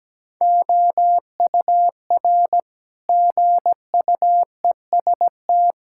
Morse code, heard as a keyed tone, about 17 wpm, 700 Hz. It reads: OUR GUEST
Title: Sentences from Top 1600 Words 17wpm